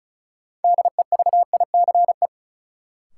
Morse code, heard as a keyed tone, about 34 wpm, 700 Hz. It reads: DEVICE
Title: Sentences from Top 1500 Words - New Word Drill - 35wpm